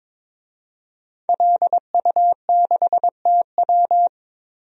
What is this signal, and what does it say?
Morse code, heard as a keyed tone, about 21 wpm, 700 Hz. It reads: LU6TW
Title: DXCC-100 Call Signs - 5 Characters - 22wpm